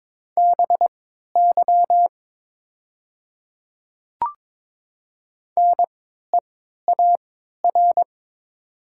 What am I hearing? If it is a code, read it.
Morse code, heard as a keyed tone, about 22 wpm, spaced as for 12 wpm, 700 Hz. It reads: BY NEAR
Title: 3x Character Spacing - Top 300 Words - 22wpm